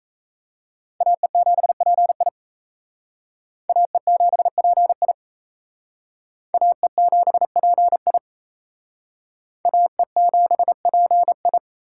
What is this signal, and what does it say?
Morse code, heard as a keyed tone, about 35 wpm, 700 Hz. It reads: AE7PI AE7PI AE7PI AE7PI
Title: Speed Racing - USA Call Signs 28wpm